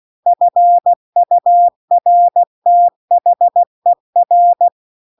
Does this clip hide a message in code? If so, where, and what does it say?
Yes: Morse code from 0.3 to 4.7 s, heard as a keyed tone, about 16 wpm, 700 Hz. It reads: FURTHER